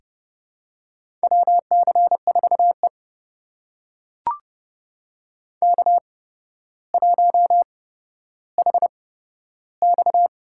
Morse code, heard as a keyed tone, about 30 wpm, spaced as for 13 wpm, 700 Hz. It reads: WC4E K1HX
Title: USA Call Signs - 4 Characters - 8x Character Spacing - 30wpm